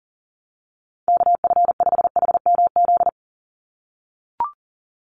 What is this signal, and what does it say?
Morse code, heard as a keyed tone, about 40 wpm, 700 Hz. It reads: KF5HMZ
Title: USA Call Signs 40wpm